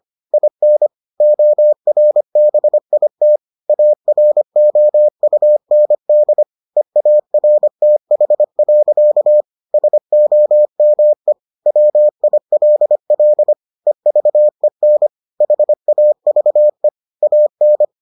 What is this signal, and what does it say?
Morse code, heard as a keyed tone, about 25 wpm, 600 Hz. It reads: IN ORBIT AROUND EARTH. SOME WILL EVEN HAVE AN